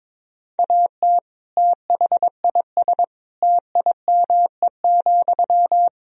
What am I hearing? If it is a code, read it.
Morse code, heard as a keyed tone, about 22 wpm, 700 Hz. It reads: AT THIS TIME,
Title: Sentences from Top 1000 Words - Review Part 2 - 22wpm